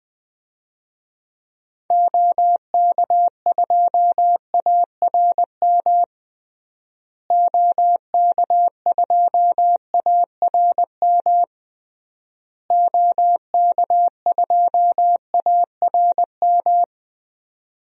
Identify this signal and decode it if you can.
Morse code, heard as a keyed tone, about 20 wpm, 700 Hz. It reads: OK2ARM OK2ARM OK2ARM
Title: DXCC-100 Call Signs - 6 Characters - Repeated 3x - 20wpm